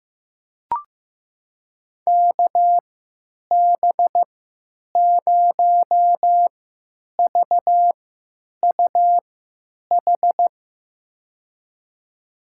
Morse code, heard as a keyed tone, about 15 wpm, spaced as for 9 wpm, 700 Hz. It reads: KB0VUH
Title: USA Call Signs - 6 Characters - 3x Character Spacing - 15wpm